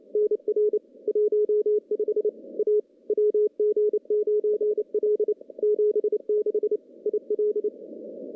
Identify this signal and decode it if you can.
Morse code, heard as a keyed tone, about 28 wpm, 420 Hz. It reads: NR 15 A WG9L 76 IL
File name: contests / CW-SS-k5tr- -1300